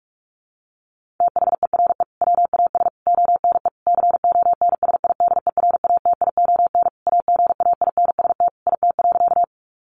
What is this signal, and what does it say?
Morse code, heard as a keyed tone, about 44 wpm, 700 Hz. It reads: THERE WAS ONE CONSIDERATION AGAINST IT.
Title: Sentences from Top 1500 Words 45wpm